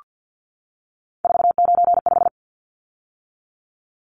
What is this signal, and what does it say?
Morse code, heard as a keyed tone, about 48 wpm, 700 Hz. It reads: 495
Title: Numbers - 3-Digit 50wpm